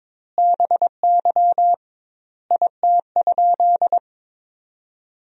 Morse code, heard as a keyed tone, about 22 wpm, 700 Hz. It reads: BY IT?
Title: 2x Word-Spacing - Sentences from Top 100 Words - 22wpm